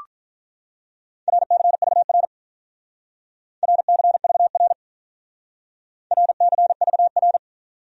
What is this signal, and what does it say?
Morse code, heard as a keyed tone, about 45 wpm, 700 Hz. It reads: RCVR RCVR RCVR